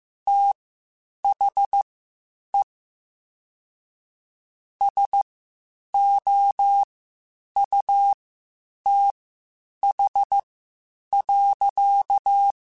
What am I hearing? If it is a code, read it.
Morse code, heard as a keyed tone, about 15 wpm, spaced as for 8 wpm, 800 Hz. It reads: THE SOUTH.
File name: Spaced Characters / Spaced Characters-mp3s / Weather 4 10 sp @13